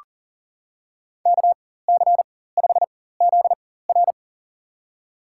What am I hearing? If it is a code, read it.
Morse code, heard as a keyed tone, about 40 wpm, spaced as for 19 wpm, 700 Hz. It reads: KC5ZR
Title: USA Call Signs - 5 Characters - 4x Character Spacing - 40wpm